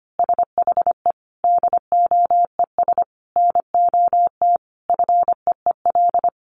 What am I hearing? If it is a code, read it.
Morse code, heard as a keyed tone, about 25 wpm, 700 Hz. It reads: SHE DOES NOT FEEL
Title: Easy Phrases 25wpm